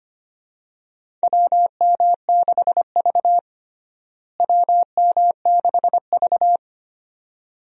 Morse code, heard as a keyed tone, about 25 wpm, 700 Hz. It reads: WM6V WM6V
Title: USA Call Signs - 4 Characters - Repeated 2x - 25wpm